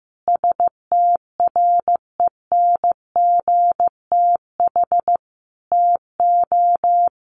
Morse code, heard as a keyed tone, about 15 wpm, 700 Hz. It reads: STRENGTH TO
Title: Sentences from Top 1400 Words 15wpm